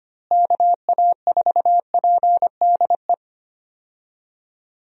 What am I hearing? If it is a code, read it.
Morse code, heard as a keyed tone, about 25 wpm, 700 Hz. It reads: KA4PDE
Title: USA Call Signs - 6 Characters - Repeated 2x - 25wpm